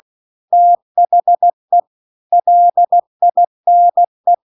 Morse code, heard as a keyed tone, about 16 wpm, 700 Hz. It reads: THE LINE